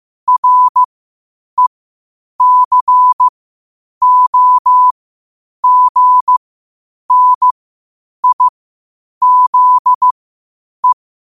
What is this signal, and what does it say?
Morse code, heard as a keyed tone, about 15 wpm, spaced as for 8 wpm, 1000 Hz. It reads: RECOGNIZE
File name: qotdfiles / 8wpm